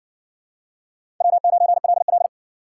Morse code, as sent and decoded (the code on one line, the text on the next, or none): .-- ----. .-.. .-..
W9LL